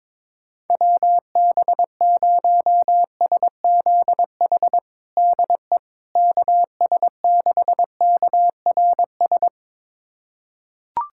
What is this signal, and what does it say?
Morse code, heard as a keyed tone, about 22 wpm, 700 Hz. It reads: WB0SZH DE KS6KRS
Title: USA Callsign DE Callsign 22wpm